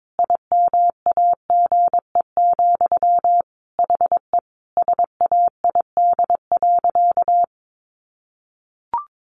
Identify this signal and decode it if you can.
Morse code, heard as a keyed tone, about 22 wpm, 700 Hz. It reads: IMAGE, HE SAID.
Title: Sentences from Top 700 Words 22wpm